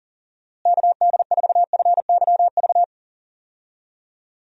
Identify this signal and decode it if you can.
Morse code, heard as a keyed tone, about 40 wpm, 700 Hz. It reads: KD4FYV
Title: USA Call Signs - 6 Characters - Repeated 3x - 40wpm